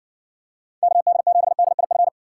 Morse code, heard as a keyed tone, about 60 wpm, 700 Hz. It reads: KD7DIF